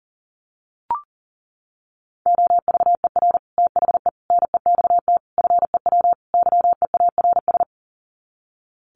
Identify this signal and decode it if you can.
Morse code, heard as a keyed tone, about 40 wpm, 700 Hz. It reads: OVER THE NEXT FEW YEARS
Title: Sets of 5 Words Encore 40wpm